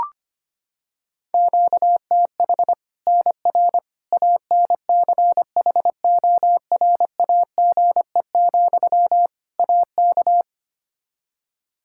Morse code, heard as a keyed tone, about 25 wpm, 700 Hz. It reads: QTH NR ANCHORAGE, AK
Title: K1USN SST QSO Practice 25wpm